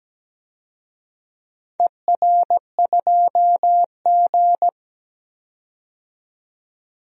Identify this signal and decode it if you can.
Morse code, heard as a keyed tone, about 17 wpm, 700 Hz. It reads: ER2G
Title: DXCC-100 Call Signs - 4 Characters - 17wpm